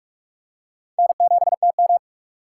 Morse code, heard as a keyed tone, about 46 wpm, 700 Hz. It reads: N7TM
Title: USA Call Signs - 4 Characters - 6x Character Spacing - 45wpm